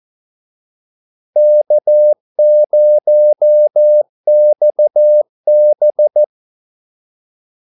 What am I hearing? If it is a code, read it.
Morse code, heard as a keyed tone, about 14 wpm, 600 Hz. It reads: K0XB
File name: Callsigns-14wpm